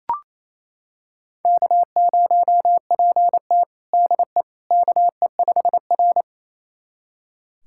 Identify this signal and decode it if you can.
Morse code, heard as a keyed tone, about 29 wpm, 690 Hz. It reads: K0PT DE KE5R
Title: USA Callsign DE Callsign - 4 Characters - 28wpm